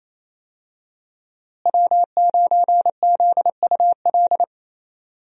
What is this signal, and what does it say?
Morse code, heard as a keyed tone, about 28 wpm, 700 Hz. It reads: W9ZUL
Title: USA Call Signs - 5 Characters - Repeated 3x - 28wpm